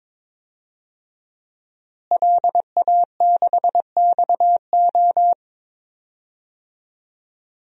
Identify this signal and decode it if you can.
Morse code, heard as a keyed tone, about 22 wpm, 700 Hz. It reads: LA6XO
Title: DXCC-100 Call Signs - 5 Characters - Repeated 3x - 22wpm